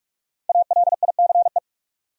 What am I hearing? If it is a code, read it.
Morse code, heard as a keyed tone, about 45 wpm, 700 Hz. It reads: ALICE
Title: USA Common Names - Max 5 Letters - 45wpm